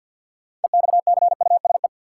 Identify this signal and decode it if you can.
Morse code, heard as a keyed tone, about 48 wpm, 700 Hz. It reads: EXCUSE
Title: Sentences from Top 700 Words - New Word Drill - 50wpm